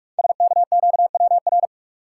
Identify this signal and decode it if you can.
Morse code, heard as a keyed tone, about 48 wpm, 700 Hz. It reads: SKQWR
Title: Code Groups - 5 Letter:Numbers - 45wpm